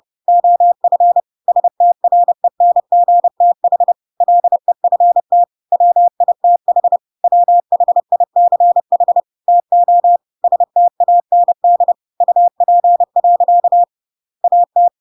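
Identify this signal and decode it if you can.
Morse code, heard as a keyed tone, about 30 wpm, 700 Hz. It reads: OF STRENGTH LEFT WITH WHICH TO STAND UP. AT
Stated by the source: Chapter 2 of The Adventures of Pinocchio read as a book in Morse code (CW) at 30 wpm, by ik1hge.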